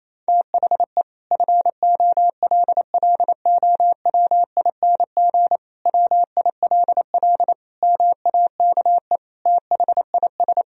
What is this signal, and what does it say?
Morse code, heard as a keyed tone, about 28 wpm, 700 Hz. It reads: THE FOLLOWING WILL MAKE THIS
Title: Sentences from Top 1000 Words - Review Part 4 - 28wpm